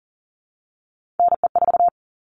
Morse code, heard as a keyed tone, about 40 wpm, 700 Hz. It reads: NE4
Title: All Cut Numbers - 3-Digit 40wpm